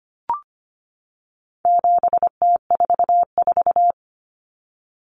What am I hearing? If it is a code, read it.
Morse code, heard as a keyed tone, about 25 wpm, 700 Hz. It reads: 7T44
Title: Common Cut Numbers - 4-Digit 25wpm